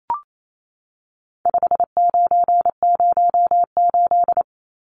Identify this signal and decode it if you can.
Morse code, heard as a keyed tone, about 28 wpm, 700 Hz. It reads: 5908
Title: Numbers - 4-Digit 28wpm